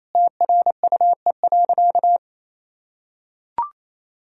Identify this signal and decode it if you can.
Morse code, heard as a keyed tone, about 28 wpm, 700 Hz. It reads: TRUE.
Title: Sentences from Top 1000 Words - Review Part 4 - 28wpm